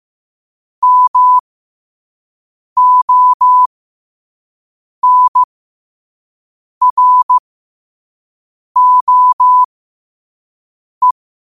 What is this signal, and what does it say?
Morse code, heard as a keyed tone, about 15 wpm, spaced as for 5 wpm, 1000 Hz. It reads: MONROE